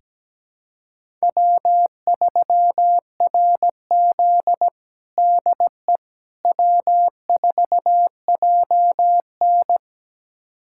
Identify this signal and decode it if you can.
Morse code, heard as a keyed tone, about 17 wpm, 700 Hz. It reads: W3RZ DE W4JN
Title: USA Callsign DE Callsign - 4 Characters - 17wpm